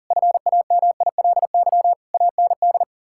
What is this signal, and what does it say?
Morse code, heard as a keyed tone, about 40 wpm, 700 Hz. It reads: FAMILY AND
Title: Sentences from Top 1000 Words - Review Part 5 - 40wpm